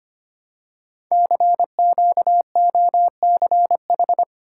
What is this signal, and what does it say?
Morse code, heard as a keyed tone, about 25 wpm, 700 Hz. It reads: CQOCH
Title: Code Groups - 5 Letter:Numbers - 25wpm